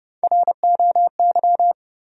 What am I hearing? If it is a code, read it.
Morse code, heard as a keyed tone, about 30 wpm, 700 Hz. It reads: ROY
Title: USA Common Names - Max 3 Letters - 30wpm